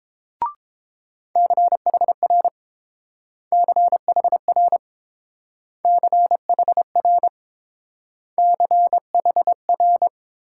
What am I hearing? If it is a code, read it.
Morse code, heard as a keyed tone, about 31 wpm, 700 Hz. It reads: CHR CHR CHR CHR